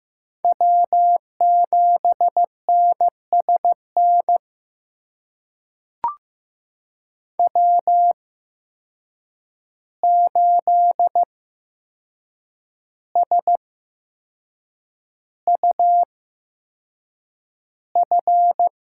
Morse code, heard as a keyed tone, about 16 wpm, spaced as for 6 wpm, 700 Hz. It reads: W7NSN W8SUF
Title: USA Call Signs - 5 Characters - 8x Character Spacing - 15wpm